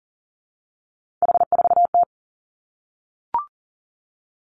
Morse code, heard as a keyed tone, about 40 wpm, 700 Hz. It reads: H4T
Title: Three Challenging Letter-Number Combinations 40wpm